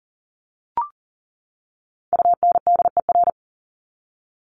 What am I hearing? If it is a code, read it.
Morse code, heard as a keyed tone, about 40 wpm, 700 Hz. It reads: UNDER
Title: Top 300 Words 40wpm